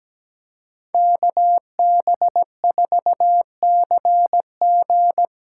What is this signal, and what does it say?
Morse code, heard as a keyed tone, about 17 wpm, 700 Hz. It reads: KB4CG
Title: K1USN SST QSO Practice - 17wpm